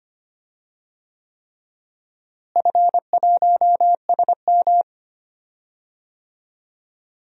Morse code, heard as a keyed tone, about 25 wpm, 700 Hz. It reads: F1SM
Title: DXCC-100 Call Signs - 4 Characters - Repeated 3x - 25wpm